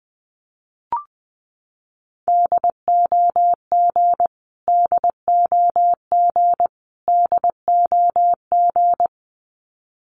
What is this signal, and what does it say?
Morse code, heard as a keyed tone, about 20 wpm, 700 Hz. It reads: DOG DOG DOG
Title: Top 500 Words Repeated 20wpm